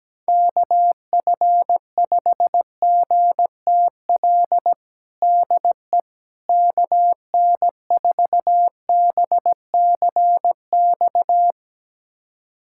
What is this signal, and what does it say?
Morse code, heard as a keyed tone, about 17 wpm, 700 Hz. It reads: KF5GTL DE KN4BCX
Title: USA Callsign DE Callsign 17wpm